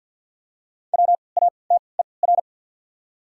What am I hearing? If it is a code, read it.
Morse code, heard as a keyed tone, about 48 wpm, 700 Hz. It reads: W A T E R
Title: 3x Character Spacing - Top 100 Words - 50wpm